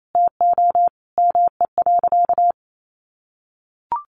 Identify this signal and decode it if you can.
Morse code, heard as a keyed tone, about 28 wpm, 700 Hz. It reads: TO ME.
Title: Sentences from Top 600 Words 28wpm